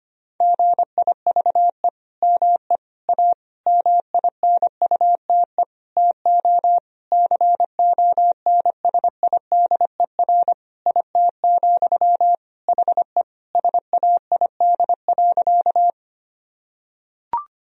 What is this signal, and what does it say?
Morse code, heard as a keyed tone, about 25 wpm, 700 Hz. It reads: GIVE ME A MINUTE TO CONSIDER IT, HE SAID.
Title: Sentences from Top 1000 Words - Review Part 4 - 25wpm